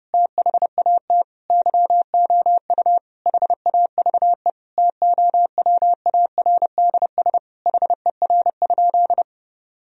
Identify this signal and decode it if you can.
Morse code, heard as a keyed tone, about 30 wpm, 700 Hz. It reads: THAT YOU HAVE TOWARDS HER?
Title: Sentences from Top 1000 Words - Review Part 4 - 30wpm